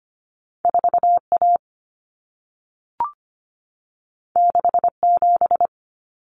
Morse code, heard as a keyed tone, about 25 wpm, 700 Hz. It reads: 4A 67
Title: Common Cut Numbers - 2-Digit 25wpm